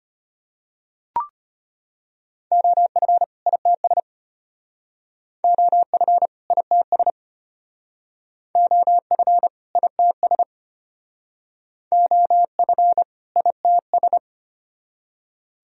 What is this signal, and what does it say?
Morse code, heard as a keyed tone, about 32 wpm, 700 Hz. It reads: OF ITS OF ITS OF ITS OF ITS